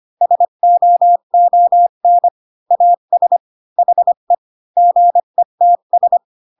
Morse code, heard as a keyed tone, about 25 wpm, 700 Hz. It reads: SOON AS HE GETS